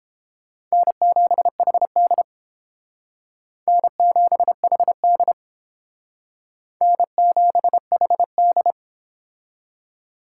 Morse code, heard as a keyed tone, about 30 wpm, 700 Hz. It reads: N7HD N7HD N7HD